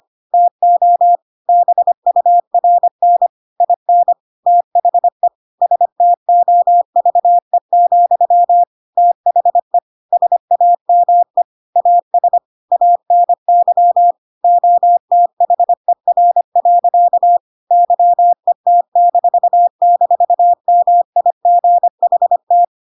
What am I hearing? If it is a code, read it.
Morse code, heard as a keyed tone, about 25 wpm, 700 Hz. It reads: TO BURN IN THE STOVE, THE SAME AS ANY OTHER. YET--MIGHT
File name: Pinocchio / Pinocchio ch1 25 25 ik1hge